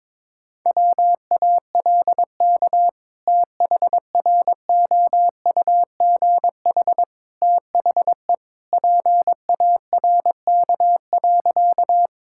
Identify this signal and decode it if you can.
Morse code, heard as a keyed tone, about 22 wpm, 700 Hz. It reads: WALK THROUGH THE PARK.
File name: practice / 22wpm / Sentences with 2-Word Phrases - 22wpm